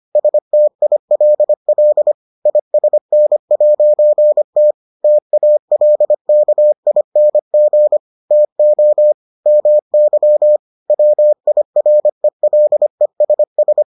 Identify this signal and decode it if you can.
Morse code, heard as a keyed tone, about 25 wpm, 600 Hz. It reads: STILL ISN'T TALKING TO MY WIRELESS